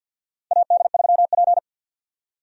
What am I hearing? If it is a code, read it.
Morse code, heard as a keyed tone, about 52 wpm, 700 Hz. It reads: AD3P
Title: USA Call Signs - 4 Characters - 50wpm